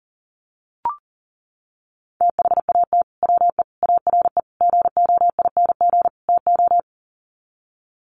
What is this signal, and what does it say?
Morse code, heard as a keyed tone, about 40 wpm, 700 Hz. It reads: THAT WE ARE GOING TO
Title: Sets of 5 Words Encore 40wpm